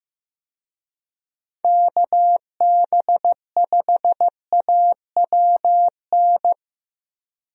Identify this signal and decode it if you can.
Morse code, heard as a keyed tone, about 15 wpm, 700 Hz. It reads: KB5AWN
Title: USA Call Signs - 6 Characters - Repeated 3x - 15wpm